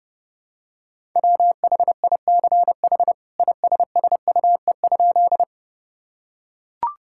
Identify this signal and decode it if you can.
Morse code, heard as a keyed tone, about 30 wpm, 700 Hz. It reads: WHICH ISSUE?
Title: Sentences from Top 1000 Words - Review Part 6 - 30wpm